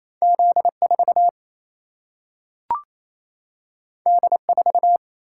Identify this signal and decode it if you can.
Morse code, heard as a keyed tone, about 28 wpm, 700 Hz. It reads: Z4 D4
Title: Single-Letter Single-Number 28wpm